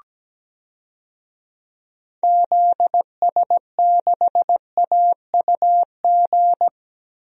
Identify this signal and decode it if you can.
Morse code, heard as a keyed tone, about 17 wpm, 700 Hz. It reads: ZS6AUG
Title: DXCC-100 Call Signs - 6 Characters - Repeated 2x - 17wpm